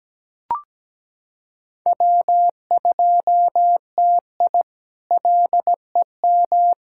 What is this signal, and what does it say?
Morse code, heard as a keyed tone, about 17 wpm, 700 Hz. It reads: W2TI LEM
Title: K1USN SST QSO Practice - 17wpm